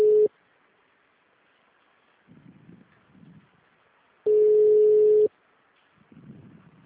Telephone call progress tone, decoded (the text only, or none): ringback tone